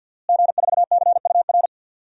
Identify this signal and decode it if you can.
Morse code, heard as a keyed tone, about 51 wpm, 700 Hz. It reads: G4CUR
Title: DXCC-100 Call Signs - 5 Characters - Repeated 2x - 50wpm